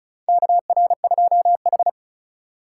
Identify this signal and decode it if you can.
Morse code, heard as a keyed tone, about 36 wpm, 700 Hz. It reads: KR2H
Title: USA Call Signs - 4 Characters - 4x Character Spacing - 35wpm